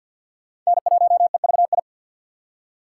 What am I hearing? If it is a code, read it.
Morse code, heard as a keyed tone, about 47 wpm, 700 Hz. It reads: N1EVI
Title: USA Call Signs - 6x Character Spacing - 50wpm